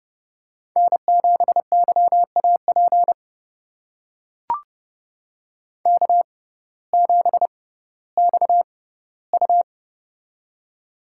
Morse code, heard as a keyed tone, about 30 wpm, spaced as for 17 wpm, 700 Hz. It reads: N7YAP K7XU
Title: USA Call Signs - 6x Character Spacing - 30wpm